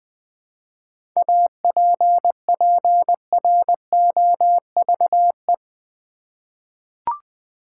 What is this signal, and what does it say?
Morse code, heard as a keyed tone, about 20 wpm, 700 Hz. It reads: APPROVE
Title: Seven Letter Words - 20wpm